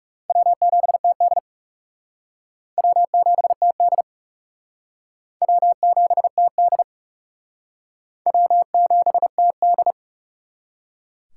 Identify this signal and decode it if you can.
Morse code, heard as a keyed tone, about 38 wpm, 700 Hz. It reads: W7TD W7TD W7TD W7TD
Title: Speed Racing - USA 1x2 Call Signs 30wpm